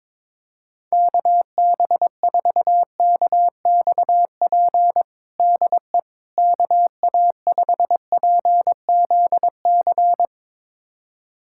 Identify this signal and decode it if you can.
Morse code, heard as a keyed tone, about 22 wpm, 700 Hz. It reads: KB4KXP DE KA5PZC
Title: USA Callsign DE Callsign - 6 Characters - 22wpm